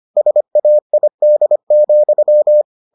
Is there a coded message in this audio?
Yes: Morse code from 0.2 to 2.6 s, heard as a keyed tone, about 25 wpm, 600 Hz. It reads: SAID,